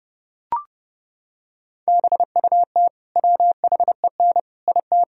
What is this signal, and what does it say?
Morse code, heard as a keyed tone, about 27 wpm, 700 Hz. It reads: BUT WHEN IT